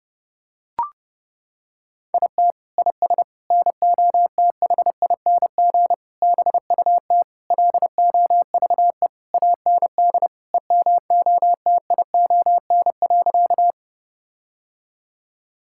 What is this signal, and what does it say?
Morse code, heard as a keyed tone, about 30 wpm, 700 Hz. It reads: IT IS NOTHING BUT LOVE AND EMOTION.
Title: Sentences from Top 1600 Words 30wpm